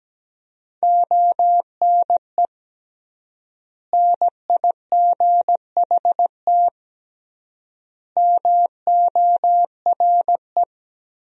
Morse code, heard as a keyed tone, about 17 wpm, 700 Hz. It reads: ONE NIGHT MORE
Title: 3x Word-Spacing - Sentences from Top 300 Words - 17wpm